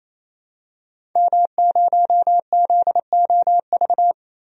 Morse code, heard as a keyed tone, about 28 wpm, 700 Hz. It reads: M0ZOV
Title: DXCC-100 Call Signs - 5 Characters - 28wpm